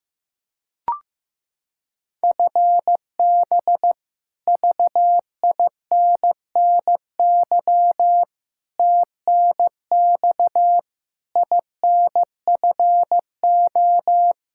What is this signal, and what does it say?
Morse code, heard as a keyed tone, about 16 wpm, 700 Hz. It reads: FB VINNY TNX INFO
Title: Practice QSO - Exchange 3 - More Info - 15wpm